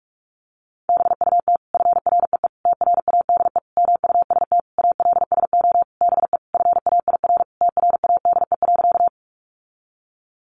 Morse code, heard as a keyed tone, about 44 wpm, 700 Hz. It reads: BUT FREE TRADE MUST ALSO BE FAIR TRADE.
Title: Sentences from Top 1300 Words 45wpm